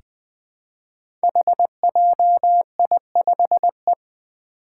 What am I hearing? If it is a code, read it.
Morse code, heard as a keyed tone, about 20 wpm, 700 Hz. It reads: HJI5E
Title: Five Challenging Letter-Number Combinations 20wpm